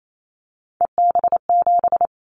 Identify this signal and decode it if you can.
Morse code, heard as a keyed tone, about 28 wpm, 700 Hz. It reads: EB7
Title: Callsign Prefixes for Top 25 DXCC - 28wpm